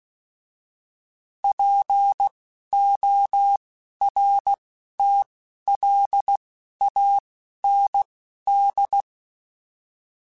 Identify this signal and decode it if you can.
Morse code, heard as a keyed tone, about 16 wpm, spaced as for 11 wpm, 800 Hz. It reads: PORTLAND